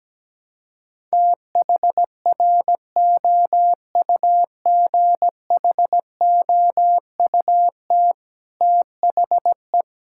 Morse code, heard as a keyed tone, about 17 wpm, 700 Hz. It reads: THROUGHOUT THE
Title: Sentences from Top 1000 Words - Review Part 5 - 17wpm